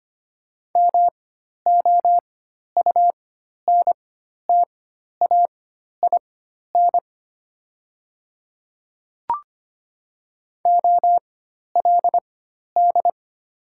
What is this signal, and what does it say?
Morse code, heard as a keyed tone, about 25 wpm, spaced as for 12 wpm, 700 Hz. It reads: MOUNTAIN OLD
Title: 4x Character Spacing - Top 300 Words - 25wpm